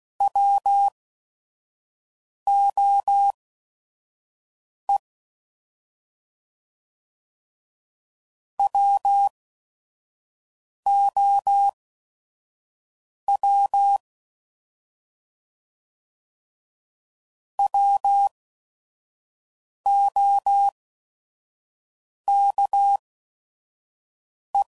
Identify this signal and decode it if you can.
Morse code, heard as a keyed tone, about 16 wpm, spaced as for 5 wpm, 800 Hz. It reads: WOE WOW WOKE
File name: Prefixes / By First Letter / Ws